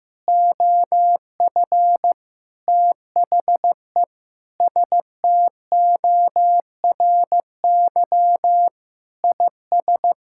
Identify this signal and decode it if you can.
Morse code, heard as a keyed tone, about 15 wpm, 700 Hz. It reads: OF THE STORY IS
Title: Sentences from Top 1000 Words - Review Part 6 - 15wpm